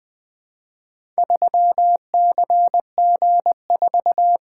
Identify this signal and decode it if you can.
Morse code, heard as a keyed tone, about 20 wpm, 700 Hz. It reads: 3CG4
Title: Four Letter-Number Combinations - 20wpm